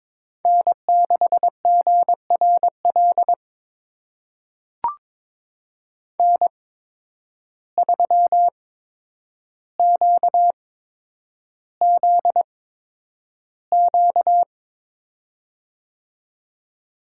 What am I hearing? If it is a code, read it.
Morse code, heard as a keyed tone, about 22 wpm, spaced as for 9 wpm, 700 Hz. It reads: N6GRL N3QZQ
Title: USA Call Signs - 5 Characters - 8x Character Spacing - 22wpm